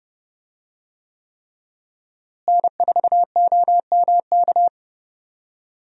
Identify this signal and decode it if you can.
Morse code, heard as a keyed tone, about 30 wpm, 700 Hz. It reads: N4OMK
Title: USA Call Signs - 5 Characters - Repeated 3x - 30wpm